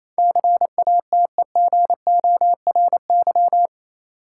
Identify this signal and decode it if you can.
Morse code, heard as a keyed tone, about 28 wpm, 700 Hz. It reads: CATEGORY
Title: Sentences from Top 1000 Words - Review Part 7 - New Word Drill - 28wpm